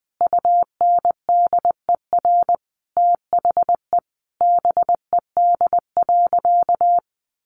Morse code, heard as a keyed tone, about 20 wpm, 700 Hz. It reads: UNDER THE BED.
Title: Easy Phrases 20wpm